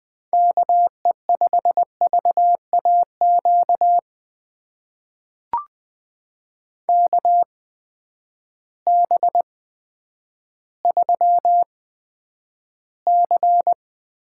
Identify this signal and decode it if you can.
Morse code, heard as a keyed tone, about 20 wpm, spaced as for 10 wpm, 700 Hz. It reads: KE5VAQ KB3C
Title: USA Call Signs - 6 Characters - 8x Character Spacing - 20wpm